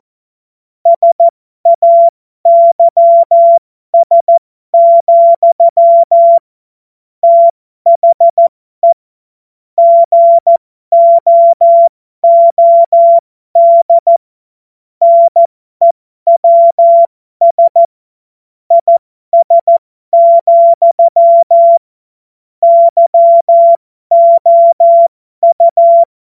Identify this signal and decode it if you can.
Morse code, heard as a keyed tone, about 14 wpm, spaced as for 12 wpm, 680 Hz. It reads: SAYS, THE GOOD NEWS IS, YOU
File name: Morsekurs / Witze